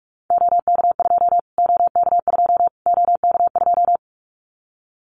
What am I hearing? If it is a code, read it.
Morse code, heard as a keyed tone, about 44 wpm, 700 Hz. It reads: OK2 OK2 OK2
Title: Callsign Prefixes for Top 25 DXCC - Repeated 3x - 45wpm